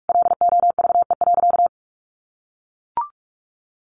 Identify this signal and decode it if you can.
Morse code, heard as a keyed tone, about 44 wpm, 700 Hz. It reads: LOVE.
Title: Sentences from Top 1300 Words 45wpm